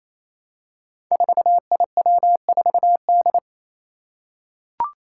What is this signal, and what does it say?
Morse code, heard as a keyed tone, about 27 wpm, 700 Hz. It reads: 4IW4D
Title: Five Challenging Letter-Number Combinations 28wpm